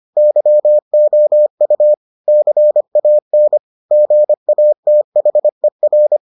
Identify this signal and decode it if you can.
Morse code, heard as a keyed tone, about 25 wpm, 600 Hz. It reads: YOU CAN GATHER